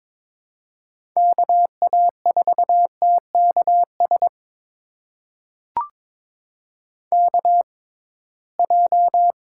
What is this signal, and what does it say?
Morse code, heard as a keyed tone, about 21 wpm, spaced as for 16 wpm, 700 Hz. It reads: KA4TKS KJ
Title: USA Call Signs - 6 Characters - 6x Character Spacing - 22wpm